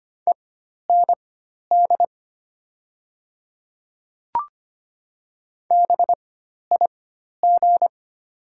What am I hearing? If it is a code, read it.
Morse code, heard as a keyed tone, about 25 wpm, spaced as for 12 wpm, 700 Hz. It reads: END BIG
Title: 4x Character Spacing - Top 300 Words - 25wpm